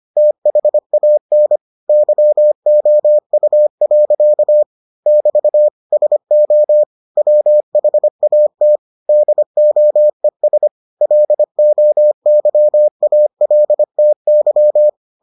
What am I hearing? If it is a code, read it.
Morse code, heard as a keyed tone, about 25 wpm, 600 Hz. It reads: THAN YOU. <BT> SO WHAT DOES LOYALTY